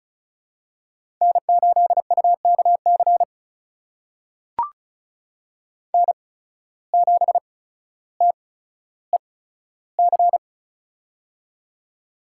Morse code, heard as a keyed tone, about 36 wpm, spaced as for 15 wpm, 700 Hz. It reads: N8UKC N7TEC
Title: USA Call Signs - 8x Character Spacing - 35wpm